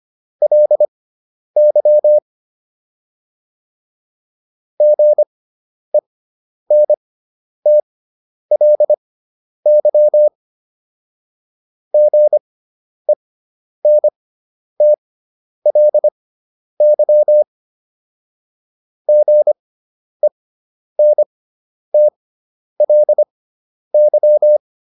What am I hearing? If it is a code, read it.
Morse code, heard as a keyed tone, about 25 wpm, spaced as for 10 wpm, 600 Hz. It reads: LY GENTLY GENTLY GENTLY